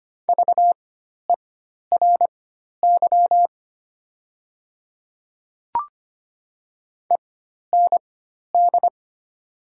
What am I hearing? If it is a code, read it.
Morse code, heard as a keyed tone, about 25 wpm, spaced as for 12 wpm, 700 Hz. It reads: VERY END
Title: 4x Character Spacing - Top 200 Words - 25wpm